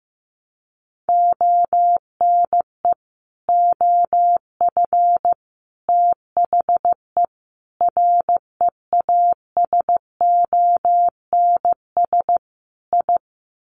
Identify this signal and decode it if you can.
Morse code, heard as a keyed tone, about 15 wpm, 700 Hz. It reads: ONE OF THE REASONS I
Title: Sets of 5 Words Encore2 15wpm